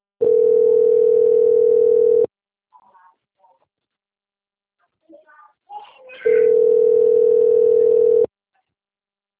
Ringback tone (North America), 440 + 480 Hz.
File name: out- - - - - .13008